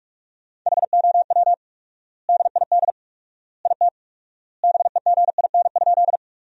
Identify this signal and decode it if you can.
Morse code, heard as a keyed tone, about 46 wpm, 700 Hz. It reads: HOW DID IT BEGIN?
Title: 4x Word-Spacing - Sentences from Top 300 Words - 45wpm